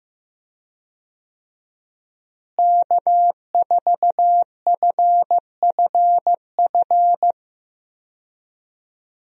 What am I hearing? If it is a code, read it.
Morse code, heard as a keyed tone, about 15 wpm, 700 Hz. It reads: K4FFF
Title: USA Call Signs - 5 Characters - Repeated 3x - 15wpm